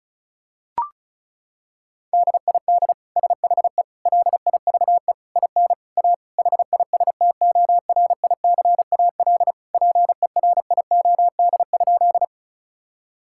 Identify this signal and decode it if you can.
Morse code, heard as a keyed tone, about 36 wpm, 700 Hz. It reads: DID SHE LIVE IN A HISTORICAL PERIOD?
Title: Sentences from Top 1000 Words - Review Part 5 - 35wpm